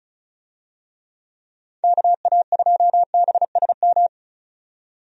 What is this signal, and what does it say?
Morse code, heard as a keyed tone, about 36 wpm, 700 Hz. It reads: KA2BSM
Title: USA Call Signs - 6 Characters - 35wpm